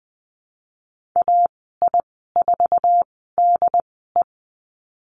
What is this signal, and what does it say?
Morse code, heard as a keyed tone, about 20 wpm, spaced as for 14 wpm, 700 Hz. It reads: AI4DE
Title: 082 Lesson 14g - Number 2 - Call Signs 20wpm